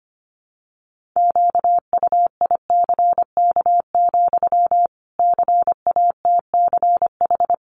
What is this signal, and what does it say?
Morse code, heard as a keyed tone, about 25 wpm, 700 Hz. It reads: QUICK, CATCH
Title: Sentences from Top 1500 Words 25wpm